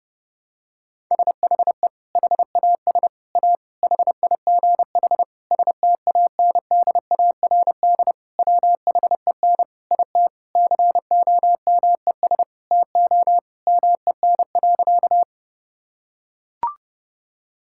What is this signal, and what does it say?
Morse code, heard as a keyed tone, about 30 wpm, 700 Hz. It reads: SHE HAS A HIGH STANDARD WHEN IT COMES TO MEN.
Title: Sentences from Top 1000 Words - Review Part 6 - 30wpm